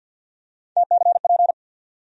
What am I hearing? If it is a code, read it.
Morse code, heard as a keyed tone, about 53 wpm, 700 Hz. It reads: TCP
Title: USA Call Sign Postfixes - 3 Letters 50wpm